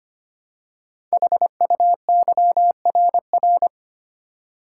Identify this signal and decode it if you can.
Morse code, heard as a keyed tone, about 25 wpm, 700 Hz. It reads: HUYRR
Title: Code Groups - 5 Letters - 25wpm